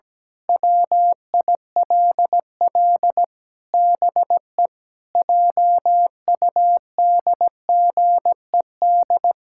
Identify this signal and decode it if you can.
Morse code, heard as a keyed tone, about 17 wpm, 700 Hz. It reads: WILL BE JUDGED
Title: Pangram Challenge v2 - 17wpm